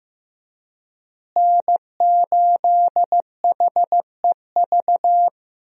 Morse code, heard as a keyed tone, about 15 wpm, 700 Hz. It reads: N8HEV
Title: USA Call Signs - 5 Characters - Repeated 3x - 15wpm